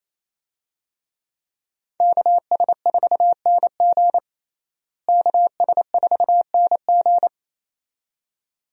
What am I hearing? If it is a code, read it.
Morse code, heard as a keyed tone, about 28 wpm, 700 Hz. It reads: KS4NG KS4NG
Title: USA Call Signs - 5 Characters - Repeated 2x - 28wpm